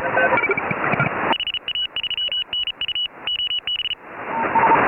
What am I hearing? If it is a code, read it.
Morse code, heard as a keyed tone, about 34 wpm, 2950 Hz, with noise. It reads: HA3NU OB
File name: AUDIO2018 / HA3FMR CW